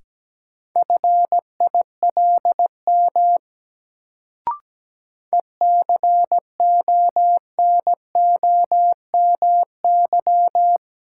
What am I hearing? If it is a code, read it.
Morse code, heard as a keyed tone, about 17 wpm, 700 Hz. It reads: FILM ECONOMY
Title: Sentences from Top 1000 Words - Review Part 3 - New Word Drill - 17wpm